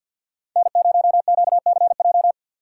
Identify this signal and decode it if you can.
Morse code, heard as a keyed tone, about 51 wpm, 700 Hz. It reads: N0QCJ
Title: USA Call Signs - 5 Characters - 3x Character Spacing - 50wpm